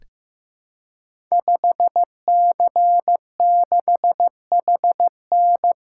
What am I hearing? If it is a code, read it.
Morse code, heard as a keyed tone, about 15 wpm, 700 Hz. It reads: 5C6HN
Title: Code Groups - 5 Letter:Numbers - 15wpm